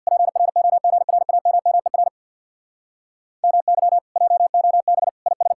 Morse code, heard as a keyed tone, about 62 wpm, 700 Hz. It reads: PROGRAMMER MY JOB IS
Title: Speed-Racing Common QSO Phrases - 50wpm